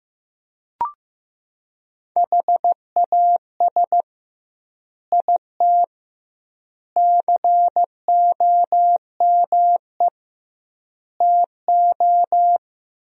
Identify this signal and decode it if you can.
Morse code, heard as a keyed tone, about 15 wpm, 700 Hz. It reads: HAS IT COME TO
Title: 2x Word-Spacing - Sentences from Top 100 Words - 15wpm